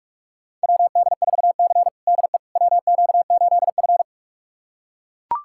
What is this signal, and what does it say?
Morse code, heard as a keyed tone, about 44 wpm, 700 Hz. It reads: WD4C DE WQ8F
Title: USA Callsign DE Callsign - 4 Characters - 45wpm